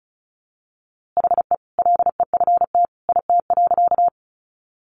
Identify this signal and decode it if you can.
Morse code, heard as a keyed tone, about 34 wpm, 700 Hz. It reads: HE LEFT IT.
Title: A Sound Of Thunder 35wpm